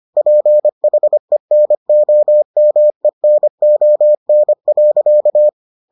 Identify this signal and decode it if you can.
Morse code, heard as a keyed tone, about 25 wpm, 600 Hz. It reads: PHENOMENON.